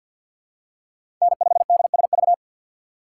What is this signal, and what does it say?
Morse code, heard as a keyed tone, about 51 wpm, 700 Hz. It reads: N5DSV
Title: USA Call Signs - 3x Character Spacing - 50wpm